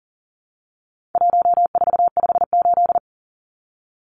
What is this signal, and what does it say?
Morse code, heard as a keyed tone, about 37 wpm, 700 Hz. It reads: 1458
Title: Numbers - 4-Digit 40wpm